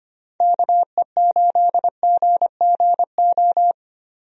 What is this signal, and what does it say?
Morse code, heard as a keyed tone, about 25 wpm, 700 Hz. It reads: KE8GGO
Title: USA Call Signs - 6 Characters - Repeated 2x - 25wpm